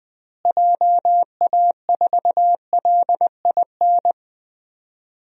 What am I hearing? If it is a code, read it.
Morse code, heard as a keyed tone, about 20 wpm, 700 Hz. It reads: JA4LIN
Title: DXCC-100 Call Signs - 6 Characters - Repeated 2x - 20wpm